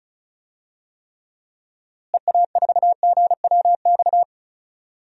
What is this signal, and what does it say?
Morse code, heard as a keyed tone, about 36 wpm, 700 Hz. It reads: EA4GWX
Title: DXCC-100 Call Signs - 6 Characters - Repeated 2x - 35wpm